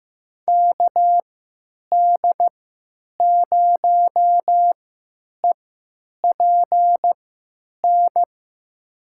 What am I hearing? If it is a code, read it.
Morse code, heard as a keyed tone, about 15 wpm, spaced as for 8 wpm, 700 Hz. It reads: KD0EPN
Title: USA Call Signs - 6 Characters - 3x Character Spacing - 15wpm